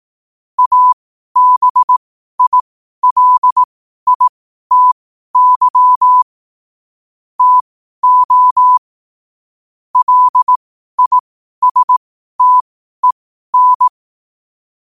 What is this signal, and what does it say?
Morse code, heard as a keyed tone, about 18 wpm, spaced as for 12 wpm, 1000 Hz. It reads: ABILITY TO LISTEN
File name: qotdfiles / 12wpm